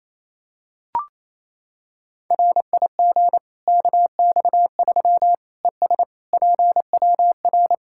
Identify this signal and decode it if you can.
Morse code, heard as a keyed tone, about 28 wpm, 700 Hz. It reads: RIG KX3 ES PWR
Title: Practice QSO - Exchange 2 - Info - 28wpm